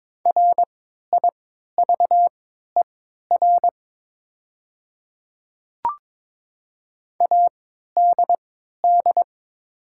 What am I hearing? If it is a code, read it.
Morse code, heard as a keyed tone, about 22 wpm, spaced as for 13 wpm, 700 Hz. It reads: RIVER ADD
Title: 3x Character Spacing - Top 300 Words - 22wpm